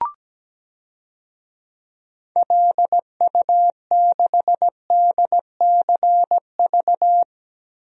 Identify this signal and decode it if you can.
Morse code, heard as a keyed tone, about 17 wpm, 700 Hz. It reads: LU6DCV
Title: DXCC-100 Call Signs - 6 Characters - Repeated 2x - 17wpm